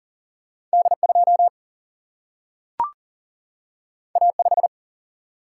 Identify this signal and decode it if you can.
Morse code, heard as a keyed tone, about 40 wpm, 700 Hz. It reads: D2 A5
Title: Single-Letter Single-Number 40wpm